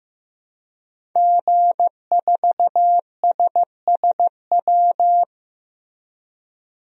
Morse code, heard as a keyed tone, about 15 wpm, 700 Hz. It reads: G4SSW
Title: DXCC-100 Call Signs - 5 Characters - Repeated 3x - 15wpm